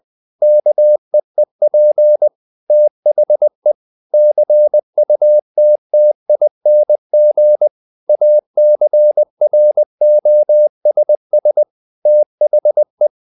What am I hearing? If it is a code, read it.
Morse code, heard as a keyed tone, about 20 wpm, 600 Hz. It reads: KEEP THE CUTTING ACROSS THE
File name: the old man and the sea part 2 20wpm